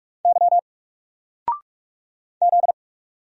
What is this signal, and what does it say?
Morse code, heard as a keyed tone, about 44 wpm, 700 Hz. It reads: Y Z
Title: Single Letter-Number Alt 45wpm